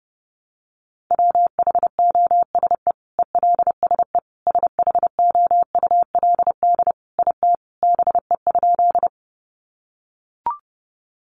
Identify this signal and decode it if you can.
Morse code, heard as a keyed tone, about 30 wpm, 700 Hz. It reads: WHOSE ELSE SHOULD IT BE?
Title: Sentences from Top 1200 Words 30wpm